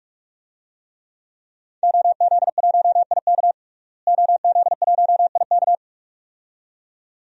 Morse code, heard as a keyed tone, about 41 wpm, 700 Hz. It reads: OZ1IK OZ1IK
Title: DXCC-100 Call Signs - 5 Characters - Repeated 2x - 45wpm